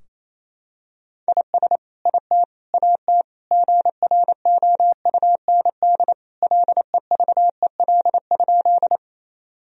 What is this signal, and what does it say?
Morse code, heard as a keyed tone, about 28 wpm, 700 Hz. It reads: IS IT AT GROUND LEVEL?
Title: Sentences from Top 1000 Words - Review Part 4 - 28wpm